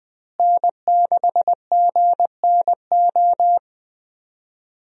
Morse code, heard as a keyed tone, about 20 wpm, 700 Hz. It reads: N6GNO
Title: USA Call Signs - 5 Characters - Repeated 3x - 20wpm